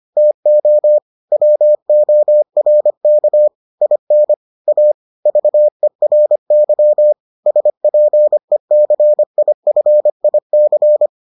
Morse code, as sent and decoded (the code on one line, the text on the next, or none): - --- / .-- --- .-. -.- / .. -. / .- / ...- . .-. -.-- / ... .--. -.-. .. ..-. .. -.-.
TO WORK IN A VERY SPECIFIC